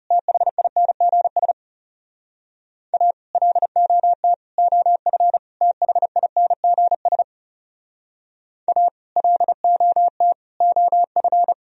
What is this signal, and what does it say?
Morse code, heard as a keyed tone, about 34 wpm, 700 Hz. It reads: THINGS A LOT OF THINGS A LOT OF
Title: Speed-Racing - Sets of 4 Words 30wpm